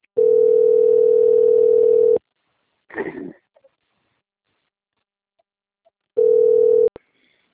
Ringback tone (North America), 440 + 480 Hz.
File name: out- - - - - .6618